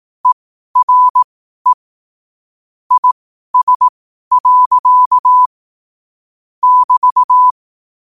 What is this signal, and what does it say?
Morse code, heard as a keyed tone, about 18 wpm, spaced as for 12 wpm, 1000 Hz. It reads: ERE IS. <BT>
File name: qotdfiles / 12wpm